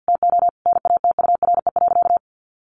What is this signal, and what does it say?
Morse code, heard as a keyed tone, about 50 wpm, 700 Hz. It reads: TO NATURE.
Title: Sentences from Top 600 Words 50wpm